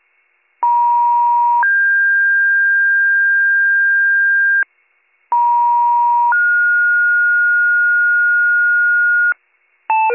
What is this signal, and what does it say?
Two-tone page, heard as a one-second tone, then a three-second tone, about 950 then 1600 Hz, about 950 then 1430 Hz, with noise. It is heard from 0.6 to 9.3 s.